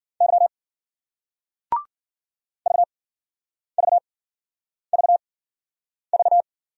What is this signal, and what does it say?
Morse code, heard as a keyed tone, about 52 wpm, spaced as for 9 wpm, 700 Hz. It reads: Y VVVV